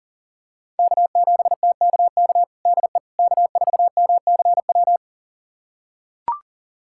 Morse code, heard as a keyed tone, about 37 wpm, 700 Hz. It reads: K7TKK DE K4MCW
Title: USA Callsign DE Callsign - 5 Characters - 40wpm